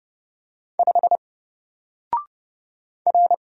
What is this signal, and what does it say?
Morse code, heard as a keyed tone, about 30 wpm, 700 Hz. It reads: 5 R
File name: Single-Letter-Number-30wpm